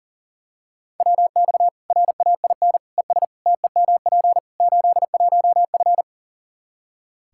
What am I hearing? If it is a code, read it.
Morse code, heard as a keyed tone, about 40 wpm, 700 Hz. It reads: WX RAIN ES TEMP 81F
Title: Practice QSO - Exchange 2 - Info - 40wpm